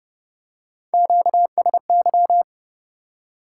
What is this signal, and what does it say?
Morse code, heard as a keyed tone, about 29 wpm, 700 Hz. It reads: QSY
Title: Top 180 QSO Elements 30wpm